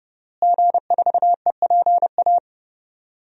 Morse code, heard as a keyed tone, about 32 wpm, 700 Hz. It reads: G4EPA
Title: DXCC-100 Call Signs - 5 Characters - Repeated 2x - 30wpm